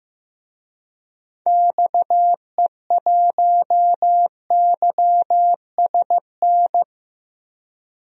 Morse code, heard as a keyed tone, about 15 wpm, 700 Hz. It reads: XE1YSN
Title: DXCC-100 Call Signs - 6 Characters - 15wpm